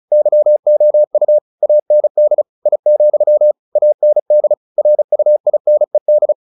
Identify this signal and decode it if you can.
Morse code, heard as a keyed tone, about 34 wpm, 600 Hz. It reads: YOU AND I, AND RUINED
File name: the old man and the sea 2 35wpm